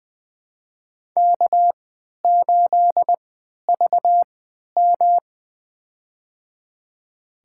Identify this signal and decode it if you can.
Morse code, heard as a keyed tone, about 20 wpm, spaced as for 11 wpm, 700 Hz. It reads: K8VM
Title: USA Call Signs - 4 Characters - 3x Character Spacing - 20wpm